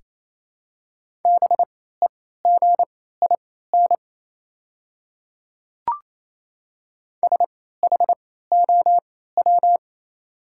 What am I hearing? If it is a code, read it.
Morse code, heard as a keyed tone, about 27 wpm, spaced as for 16 wpm, 690 Hz. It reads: BEGIN SHOW
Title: 3x Character Spacing - Top 300 Words - 28wpm